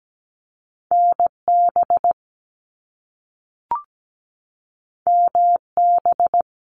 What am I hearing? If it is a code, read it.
Morse code, heard as a keyed tone, about 17 wpm, 700 Hz. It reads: NB MB
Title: Canadian Provinces and Territories 17wpm